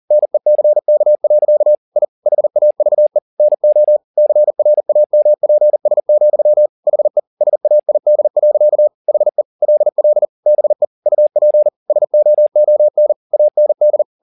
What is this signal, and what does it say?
Morse code, heard as a keyed tone, about 40 wpm, 600 Hz. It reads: NECK. I HAVE NO CRAMPS, HE SAID. HE LL BE UP SOON AND
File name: the old man and the sea 2 40wpm